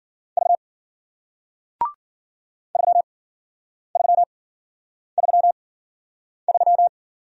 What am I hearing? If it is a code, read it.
Morse code, heard as a keyed tone, about 53 wpm, spaced as for 9 wpm, 700 Hz. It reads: V 3333